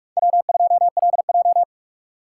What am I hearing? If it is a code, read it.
Morse code, heard as a keyed tone, about 47 wpm, 700 Hz. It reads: W2LJ
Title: USA Call Signs - 4x Character Spacing - 45wpm